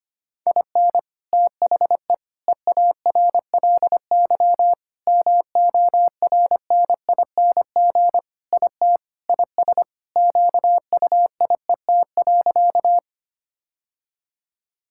Morse code, heard as a keyed tone, about 25 wpm, 700 Hz. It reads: IN THE EARLY MORNING IT IS QUIET.
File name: practice / 25wpm / Sentences with 3-Word Phrases - 25wpm